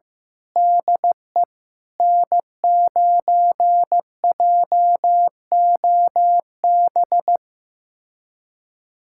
Morse code, heard as a keyed tone, about 15 wpm, 700 Hz. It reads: DE N9JOB
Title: USA Callsign DE Callsign - 5 Characters - 15wpm